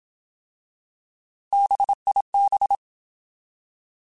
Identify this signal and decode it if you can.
Morse code, heard as a keyed tone, about 27 wpm, 800 Hz. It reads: BIB
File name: Alphabetical / Alphabetical mp3s / Bs @25